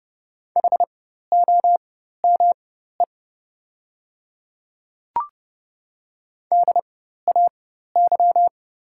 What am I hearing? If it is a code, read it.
Morse code, heard as a keyed tone, about 30 wpm, spaced as for 14 wpm, 700 Hz. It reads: HOME DAY
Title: 4x Character Spacing - Top 300 Words - 30wpm